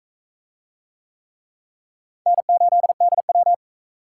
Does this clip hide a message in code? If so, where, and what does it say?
Yes: Morse code from 2.3 to 3.5 s, heard as a keyed tone, about 44 wpm, 700 Hz. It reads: N8DW